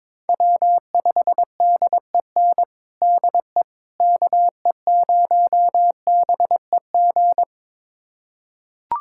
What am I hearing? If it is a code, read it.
Morse code, heard as a keyed tone, about 22 wpm, 700 Hz. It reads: W5DEN DE KE0BEG
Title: USA Callsign DE Callsign 22wpm